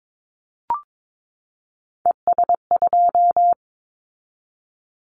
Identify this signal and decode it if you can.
Morse code, heard as a keyed tone, about 22 wpm, 700 Hz. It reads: ES2
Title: Callsign Prefixes for Top 50 DXCC - 22wpm